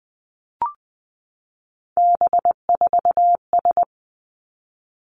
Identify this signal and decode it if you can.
Morse code, heard as a keyed tone, about 20 wpm, 700 Hz. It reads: B4S
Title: Three Challenging Letter-Number Combinations 20wpm